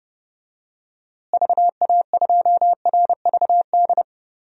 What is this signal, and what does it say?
Morse code, heard as a keyed tone, about 30 wpm, 700 Hz. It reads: VA2RVD
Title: DXCC-100 Call Signs - 6 Characters - Repeated 2x - 30wpm